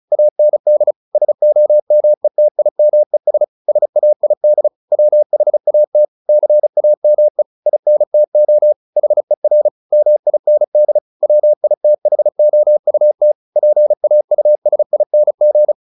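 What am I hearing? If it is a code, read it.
Morse code, heard as a keyed tone, about 34 wpm, 600 Hz. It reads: AND SOMETIMES SAID WHAT CAME INTO HER MIND WITHOUT PAUSING
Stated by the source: Morse practice at 35 wpm from CW med Gnister, sending Ken Follett's novel Never in English.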